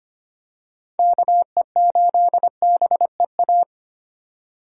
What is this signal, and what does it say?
Morse code, heard as a keyed tone, about 25 wpm, 700 Hz. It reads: KE8BEA
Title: USA Call Signs - 4x Character Spacing - 25wpm